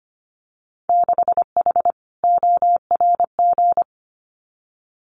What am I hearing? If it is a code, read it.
Morse code, heard as a keyed tone, about 25 wpm, 700 Hz. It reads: 6H ORG
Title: Winter Field Day Exchanges Encore 25wpm